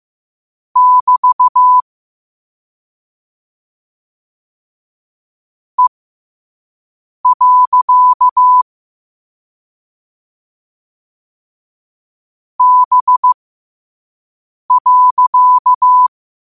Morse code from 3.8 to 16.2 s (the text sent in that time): E. B.